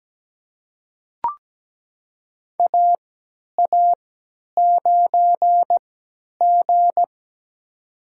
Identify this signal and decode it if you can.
Morse code, heard as a keyed tone, about 17 wpm, spaced as for 10 wpm, 700 Hz. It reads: AA9G
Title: USA Call Signs - 4 Characters - 3x Character Spacing - 17wpm